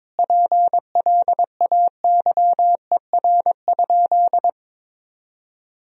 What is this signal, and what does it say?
Morse code, heard as a keyed tone, about 22 wpm, 700 Hz. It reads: PLAYER?
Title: Sentences from Top 1000 Words - Review Part 4 - 22wpm